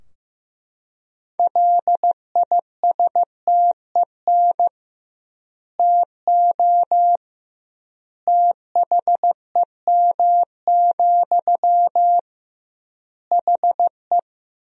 Morse code, heard as a keyed tone, about 15 wpm, 700 Hz. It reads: LISTEN TO THEM, HE
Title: 2x Word-Spacing - Sentences from Top 400 Words - 15wpm